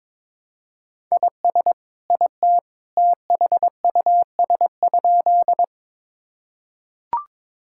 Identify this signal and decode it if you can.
Morse code, heard as a keyed tone, about 22 wpm, 700 Hz. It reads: IS IT THUS?
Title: Sentences from Top 1000 Words - Review Part 5 - 22wpm